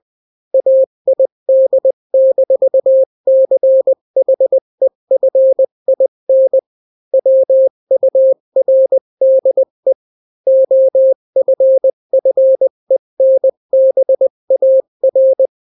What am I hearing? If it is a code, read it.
Morse code, heard as a keyed tone, about 20 wpm, 530 Hz. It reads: AID-CHEFIN WURDE OFFENBAR
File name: daily / 18wpm